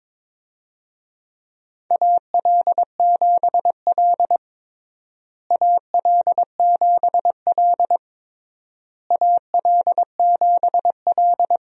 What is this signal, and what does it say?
Morse code, heard as a keyed tone, about 22 wpm, 700 Hz. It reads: AL7L AL7L AL7L
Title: USA Call Signs - 4 Characters - Repeated 3x - 22wpm